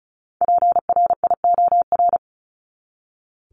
Morse code, heard as a keyed tone, about 34 wpm, 700 Hz. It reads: PRIOR
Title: Top 1501-1750 Words 35wpm